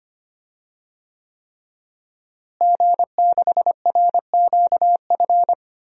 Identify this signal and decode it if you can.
Morse code, heard as a keyed tone, about 25 wpm, 700 Hz. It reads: G6RQF
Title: DXCC-100 Call Signs - 5 Characters - Repeated 3x - 25wpm